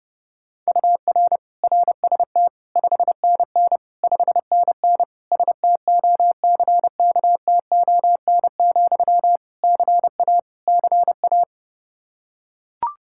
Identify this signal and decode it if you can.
Morse code, heard as a keyed tone, about 30 wpm, 700 Hz. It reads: UR RST 5NN 5NN STOCKTON, CA CA
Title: Practice QSO - Exchange 1 - Beginning - 30wpm